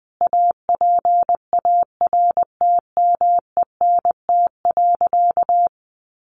Morse code, heard as a keyed tone, about 20 wpm, 700 Hz. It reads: APARTMENT.
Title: Sentences from Top 900 Words 20wpm